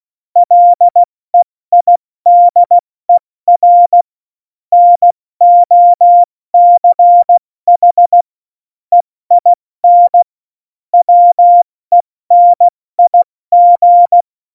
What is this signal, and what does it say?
Morse code, heard as a keyed tone, about 16 wpm, spaced as for 14 wpm, 700 Hz. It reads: LEIDER NOCH EIN WENIG
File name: Morsekurs / Witze / Witz 16. Juli 2019, 70 F80